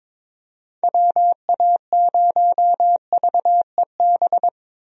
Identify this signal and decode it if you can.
Morse code, heard as a keyed tone, about 22 wpm, 700 Hz. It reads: WA0VEB
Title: USA Call Signs - 6 Characters - 22wpm